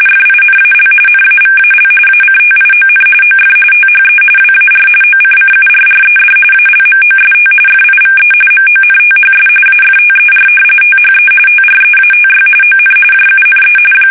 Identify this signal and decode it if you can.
RTTY, about 45.45 baud. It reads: FINAL NAVAL ACTION OF THE WAR OCCURRED ALMOST FIVE MONTHS AFTER THE TREATY ON JUNE
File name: fsk850 / usnavy2